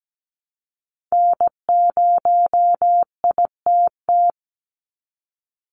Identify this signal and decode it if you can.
Morse code, heard as a keyed tone, about 17 wpm, 700 Hz. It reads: N0ITT
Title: USA Call Signs 17wpm